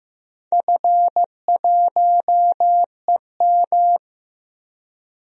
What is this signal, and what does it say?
Morse code, heard as a keyed tone, about 15 wpm, 700 Hz. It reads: F1EM
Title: DXCC-100 Call Signs - 4 Characters - Repeated 3x - 15wpm